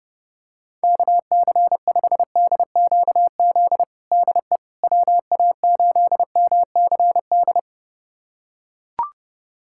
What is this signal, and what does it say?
Morse code, heard as a keyed tone, about 32 wpm, 700 Hz. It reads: KC5DQZ DE WA8MCD
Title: USA Callsign DE Callsign 30wpm